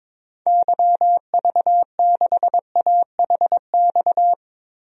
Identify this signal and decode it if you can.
Morse code, heard as a keyed tone, about 22 wpm, 700 Hz. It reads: YV6AHX
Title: DXCC-100 Call Signs - 6 Characters - Repeated 3x - 22wpm